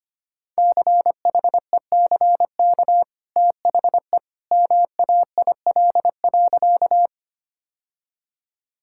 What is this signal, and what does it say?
Morse code, heard as a keyed tone, about 25 wpm, 700 Hz. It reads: CHECK THE MAIL.
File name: Easy Sentences with Hints - 25wpm